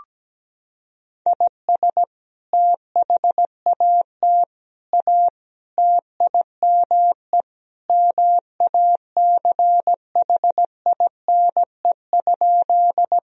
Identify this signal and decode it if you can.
Morse code, heard as a keyed tone, about 17 wpm, 700 Hz. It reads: IS THAT A TIME MACHINE?
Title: Sentences from Top 1000 Words - Review Part 2 - 17wpm